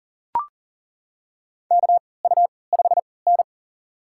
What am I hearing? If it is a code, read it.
Morse code, heard as a keyed tone, about 40 wpm, spaced as for 23 wpm, 700 Hz. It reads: KU5N
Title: USA Call Signs - 4 Characters - 3x Character Spacing - 40wpm